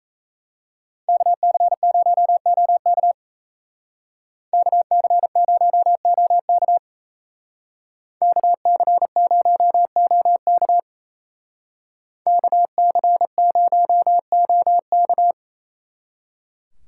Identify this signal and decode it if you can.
Morse code, heard as a keyed tone, about 32 wpm, 700 Hz. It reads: KC0OK KC0OK KC0OK KC0OK
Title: Speed Racing - USA 2x2 Call Signs 28wpm